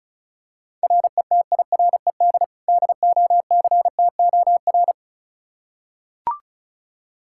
Morse code, heard as a keyed tone, about 34 wpm, 700 Hz. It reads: RETIRED DOCTOR
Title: Practice QSO - Exchange 3 - More Info - 35wpm